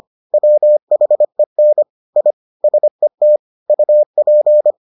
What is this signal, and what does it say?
Morse code, heard as a keyed tone, about 25 wpm, 600 Hz. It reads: WHEN I SET UP